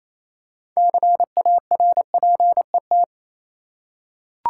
Morse code, heard as a keyed tone, about 28 wpm, 700 Hz. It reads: CARPET
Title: Six Letter Words - 28wpm